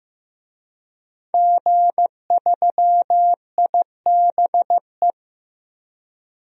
Morse code, heard as a keyed tone, about 15 wpm, 700 Hz. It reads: G3IBE
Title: DXCC-100 Call Signs - 5 Characters - 15wpm